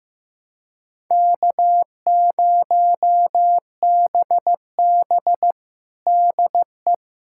Morse code, heard as a keyed tone, about 15 wpm, 700 Hz. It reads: K0BB DE
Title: USA Callsign DE Callsign - 4 Characters - 15wpm